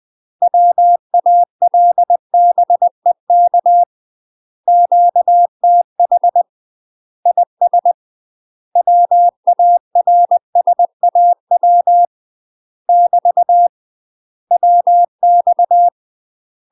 Morse code, heard as a keyed tone, about 20 wpm, 700 Hz. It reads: WALBEK QTH IS WARSAW <BT> WX
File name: UK MASTER QSOs TRAINING / 20WPM / UK MASTER QSOs 0063